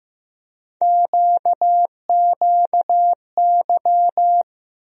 Morse code, heard as a keyed tone, about 15 wpm, 700 Hz. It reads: QQY
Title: USA Call Sign Postfixes - 3 Letters 15wpm